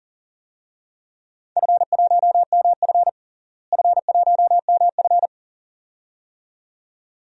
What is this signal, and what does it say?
Morse code, heard as a keyed tone, about 40 wpm, 700 Hz. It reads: F1MF F1MF
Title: DXCC-100 Call Signs - 4 Characters - Repeated 2x - 40wpm